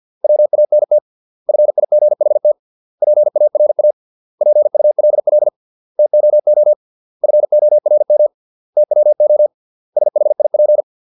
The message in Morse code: WANT FIGHT PARA PULL TOO FORM TWO SHIP

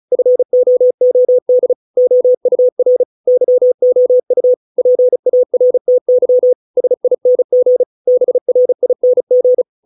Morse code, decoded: FOOD OUR YOU PARTY SING BRING

